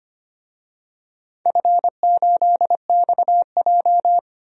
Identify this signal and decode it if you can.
Morse code, heard as a keyed tone, about 25 wpm, 700 Hz. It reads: F8XJ
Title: DXCC-100 Call Signs - 4 Characters - 25wpm